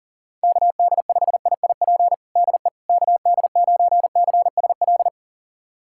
Morse code, heard as a keyed tone, about 37 wpm, 700 Hz. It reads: KD5IIP DE KD9CSL
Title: USA Callsign DE Callsign - 6 Characters - 40wpm